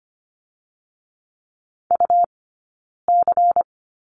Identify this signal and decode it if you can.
Morse code, heard as a keyed tone, about 25 wpm, 700 Hz. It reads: U C
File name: Lesson 17d - Letter u - Two Characters 25wpm